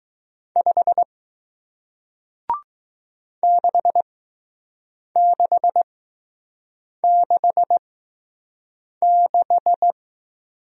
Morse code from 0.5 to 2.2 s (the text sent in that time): 5